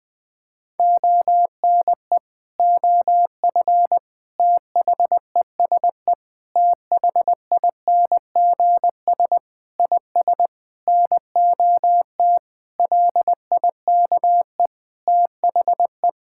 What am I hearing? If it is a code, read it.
Morse code, heard as a keyed tone, about 20 wpm, 700 Hz. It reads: ONE OF THESE THINGS IS NOT LIKE THE
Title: Sentences with 8-Word Phrases - 20wpm